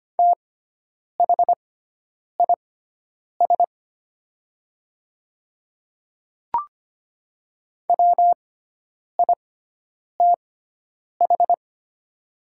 Morse code, heard as a keyed tone, about 25 wpm, spaced as for 9 wpm, 700 Hz. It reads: THIS WITH
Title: 6x Character Spacing - Top 100 Words - 25wpm